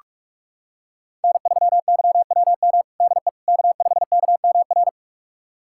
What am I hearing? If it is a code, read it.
Morse code, heard as a keyed tone, about 46 wpm, 700 Hz. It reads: N3YWM DE K5KMR
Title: USA Callsign DE Callsign - 5 Characters - 45wpm